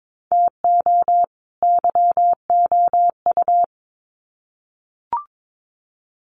Morse code, heard as a keyed tone, about 22 wpm, 700 Hz. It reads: TO YOU
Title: Sets of 2 Words 22wpm